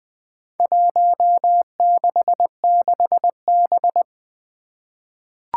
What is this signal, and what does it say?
Morse code, heard as a keyed tone, about 20 wpm, 700 Hz. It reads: 166B
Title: Four Challenging Letter-Number Combinations 20wpm